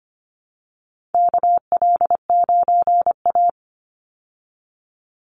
Morse code, heard as a keyed tone, about 25 wpm, 700 Hz. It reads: KL9A
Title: Call Signs Repeated 2x - 25wpm